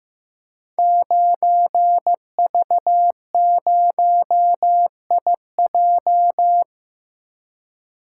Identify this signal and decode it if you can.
Morse code, heard as a keyed tone, about 15 wpm, 700 Hz. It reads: 9V0IJ
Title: Code Groups - 5 Letter:Numbers - 15wpm